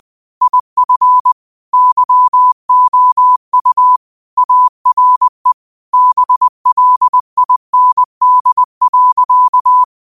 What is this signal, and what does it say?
Morse code, heard as a keyed tone, about 20 wpm, 1000 Hz. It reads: IF YOU ARE BLIND.
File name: qotdfiles / 20wpm